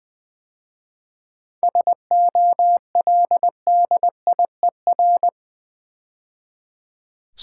Morse code, heard as a keyed tone, about 21 wpm, 700 Hz. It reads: SOLDIER
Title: 7 Letter Words - First 3 Letters Given - 20wpm